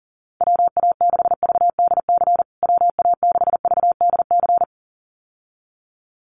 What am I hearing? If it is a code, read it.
Morse code, heard as a keyed tone, about 40 wpm, 700 Hz. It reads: WA6VDC WA6VDC
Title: USA Call Signs Repeated 2x 40wpm